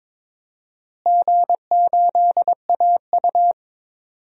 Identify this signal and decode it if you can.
Morse code, heard as a keyed tone, about 22 wpm, 700 Hz. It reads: G8AU
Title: DXCC-100 Call Signs - 4 Characters - Repeated 2x - 22wpm